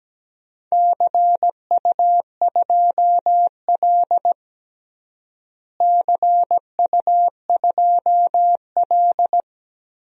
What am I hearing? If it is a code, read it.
Morse code, heard as a keyed tone, about 17 wpm, 700 Hz. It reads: CU2L CU2L
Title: DXCC-100 Call Signs - 4 Characters - Repeated 3x - 17wpm